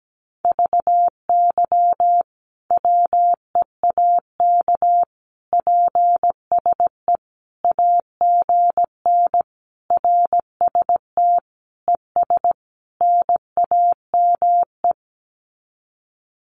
Morse code, heard as a keyed tone, about 17 wpm, 700 Hz. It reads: VY WEAK PSE AGN RST ES NAME
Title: Common QSO Phrases 17wpm